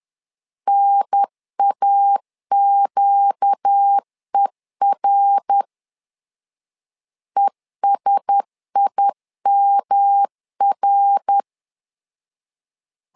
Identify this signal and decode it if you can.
Morse code, heard as a keyed tone, about 10 wpm, 800 Hz. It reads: NAQER ESIMR
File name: LEZIONI / 4 Q1S 10wpm 2133